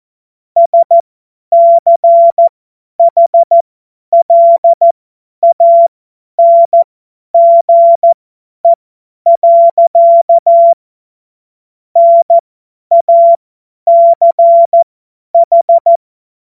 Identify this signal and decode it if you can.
Morse code, heard as a keyed tone, about 14 wpm, spaced as for 10 wpm, 680 Hz. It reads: SCHLANGE. NACH
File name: Morsekurs / Witze / Witz 25. Oktober 2022 50 F70